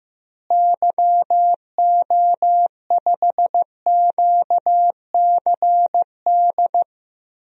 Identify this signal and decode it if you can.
Morse code, heard as a keyed tone, about 15 wpm, 700 Hz. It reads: YO5QCD
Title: DXCC-100 Call Signs - 6 Characters - Repeated 2x - 15wpm